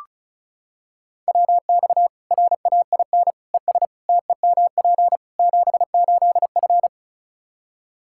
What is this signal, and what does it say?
Morse code, heard as a keyed tone, about 36 wpm, 700 Hz. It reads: WX RAIN ES TEMP 78F
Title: K1USN SST QSO Practice 35wpm